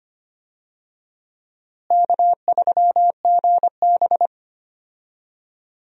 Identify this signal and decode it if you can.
Morse code, heard as a keyed tone, about 25 wpm, 700 Hz. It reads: K3GB
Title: USA Call Signs - 4 Characters - Repeated 3x - 25wpm